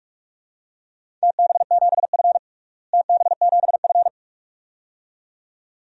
Morse code, heard as a keyed tone, about 46 wpm, 700 Hz. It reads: TB7F TB7F
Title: DXCC-100 Call Signs - 4 Characters - Repeated 2x - 45wpm